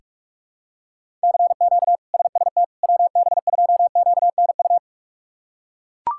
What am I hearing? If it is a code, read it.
Morse code, heard as a keyed tone, about 42 wpm, 700 Hz. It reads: CQ SST WB2QNU
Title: K1USN SST QSO Practice - 45wpm